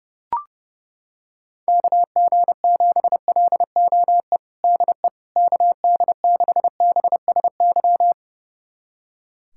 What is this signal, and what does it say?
Morse code, heard as a keyed tone, about 30 wpm, 700 Hz. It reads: KG7LOE DE KD6BSY
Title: USA Callsign DE Callsign - 6 Characters - 30wpm